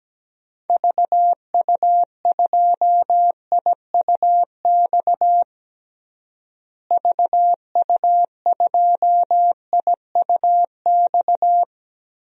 Morse code, heard as a keyed tone, about 17 wpm, 700 Hz. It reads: VU2IUX VU2IUX
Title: DXCC-100 Call Signs - 6 Characters - Repeated 2x - 17wpm